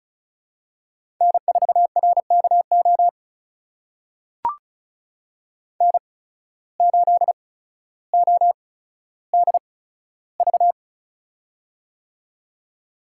Morse code, heard as a keyed tone, about 36 wpm, spaced as for 15 wpm, 700 Hz. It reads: N4RKO N8ODV
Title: USA Call Signs - 5 Characters - 8x Character Spacing - 35wpm